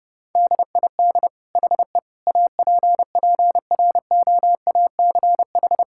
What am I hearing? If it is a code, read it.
Morse code, heard as a keyed tone, about 32 wpm, 700 Hz. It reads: DID HE APPROACH
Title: Sentences from Top 1000 Words - Review Part 6 - 30wpm